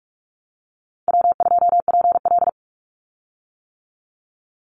Morse code, heard as a keyed tone, about 44 wpm, 700 Hz. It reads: W2PL
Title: Call Signs - 45wpm